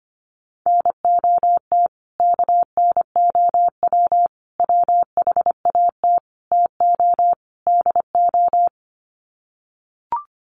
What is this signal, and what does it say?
Morse code, heard as a keyed tone, about 25 wpm, 700 Hz. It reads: NOT KNOW WHAT TO DO
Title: Sets of 5 Words 25wpm